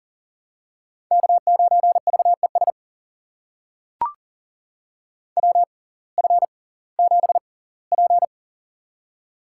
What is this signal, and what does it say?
Morse code, heard as a keyed tone, about 37 wpm, spaced as for 21 wpm, 700 Hz. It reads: K9VES WF7P
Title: USA Call Signs - 6x Character Spacing - 40wpm